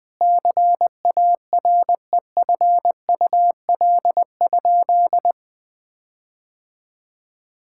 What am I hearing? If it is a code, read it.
Morse code, heard as a keyed tone, about 20 wpm, 700 Hz. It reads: CAREFUL?
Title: Sentences from Top 1600 Words 20wpm